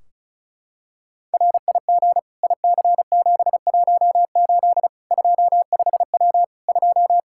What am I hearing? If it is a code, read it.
Morse code, heard as a keyed tone, about 36 wpm, 700 Hz. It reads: RIG IC718 25W 2
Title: Practice QSO - Exchange 2 - Info - 35wpm